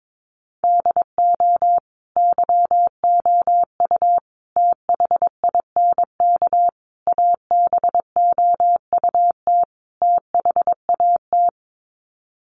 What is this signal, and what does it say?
Morse code, heard as a keyed tone, about 22 wpm, 700 Hz. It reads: DO YOU THINK ABOUT THAT
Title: Sets of 5 Words Encore2 22wpm